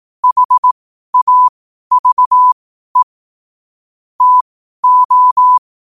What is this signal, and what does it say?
Morse code, heard as a keyed tone, about 18 wpm, spaced as for 12 wpm, 1000 Hz. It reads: HAVE TO
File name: qotdfiles / 12wpm